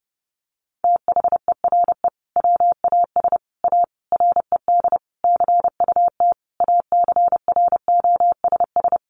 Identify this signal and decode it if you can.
Morse code, heard as a keyed tone, about 30 wpm, 700 Hz. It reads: THERE WAS A RED CUT ACROSS
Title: Sentences from Top 700 Words 30wpm